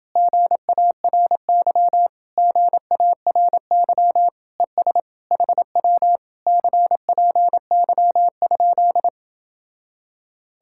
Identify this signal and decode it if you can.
Morse code, heard as a keyed tone, about 29 wpm, 700 Hz. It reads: GARY GARY ES HW CPY?